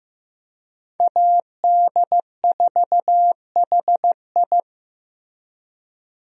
Morse code, heard as a keyed tone, about 15 wpm, 700 Hz. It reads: AD4HI
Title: USA Call Signs - 5 Characters - Repeated 3x - 15wpm